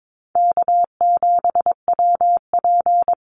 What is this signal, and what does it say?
Morse code, heard as a keyed tone, about 22 wpm, 700 Hz. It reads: K7WP
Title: Easy Call Signs Repeated 3x - 22wpm